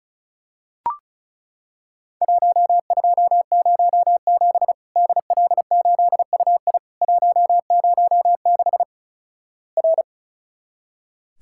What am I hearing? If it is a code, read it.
Morse code, heard as a keyed tone, about 36 wpm, 700 Hz. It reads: 1207 DL8UI 106 R
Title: Worked All Europe DX Contest - QTC Exchange - 35wpm